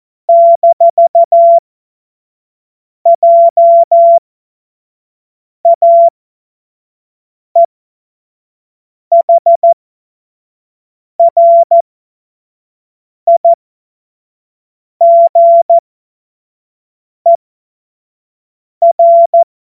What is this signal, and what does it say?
Morse code, heard as a keyed tone, about 14 wpm, spaced as for 5 wpm, 680 Hz. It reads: -JAEHRIGER
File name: Morsekurs / Witze / Witz 11. Mai 2021, 25 F70